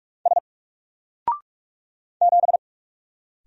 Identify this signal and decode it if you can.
Morse code, heard as a keyed tone, about 44 wpm, 700 Hz. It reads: S 7
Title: Single Letter-Number Alt 45wpm